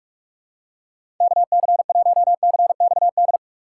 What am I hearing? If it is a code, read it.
Morse code, heard as a keyed tone, about 44 wpm, 700 Hz. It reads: KC1CXD
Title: USA Call Signs - 6 Characters - 2x Character Spacing - 45wpm